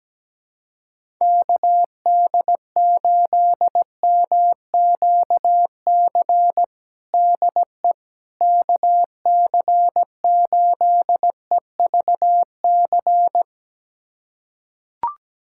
Morse code, heard as a keyed tone, about 17 wpm, 700 Hz. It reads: KD8MQC DE KC8EVC
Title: USA Callsign DE Callsign 17wpm